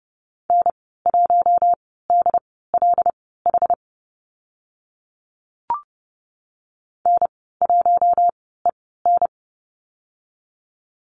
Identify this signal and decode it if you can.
Morse code, heard as a keyed tone, about 30 wpm, spaced as for 17 wpm, 700 Hz. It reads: N1DLH N1EN
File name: Lesson 14g - Number 2 - Call Signs 30wpm